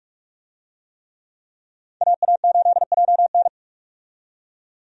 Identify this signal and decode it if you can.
Morse code, heard as a keyed tone, about 46 wpm, 700 Hz. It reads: AA8JN
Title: USA Call Signs - 5 Characters - 45wpm